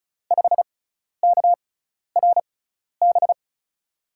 Morse code, heard as a keyed tone, about 36 wpm, spaced as for 12 wpm, 700 Hz. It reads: 5KRB
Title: USA Call Signs - 6 Characters - 6x Character Spacing - 35wpm